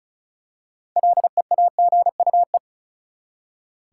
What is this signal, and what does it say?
Morse code, heard as a keyed tone, about 36 wpm, 700 Hz. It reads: LEAGUE
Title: Six Letter Words - 35wpm